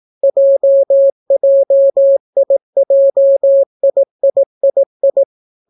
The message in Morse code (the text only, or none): JJIJIIII